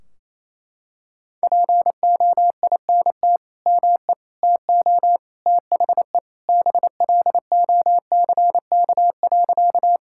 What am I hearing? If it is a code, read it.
Morse code, heard as a keyed tone, about 28 wpm, 700 Hz. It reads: POINT ME TO THE BLOCK.
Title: Sentences from Top 1600 Words 28wpm